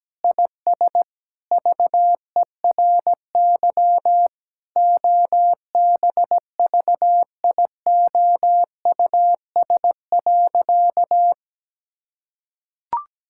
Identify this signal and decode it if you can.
Morse code, heard as a keyed tone, about 17 wpm, 700 Hz. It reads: IS VERY OBVIOUS.
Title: Sentences from Top 1600 Words 17wpm